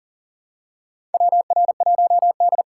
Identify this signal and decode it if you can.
Morse code, heard as a keyed tone, about 40 wpm, 700 Hz. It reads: WR1D
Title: USA Call Signs - 2x Character Spacing - 40wpm